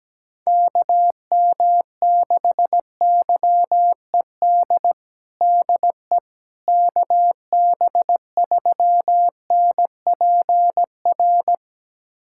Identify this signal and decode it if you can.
Morse code, heard as a keyed tone, about 17 wpm, 700 Hz. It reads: KM6YED DE KB3NPR
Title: USA Callsign DE Callsign - 6 Characters - 17wpm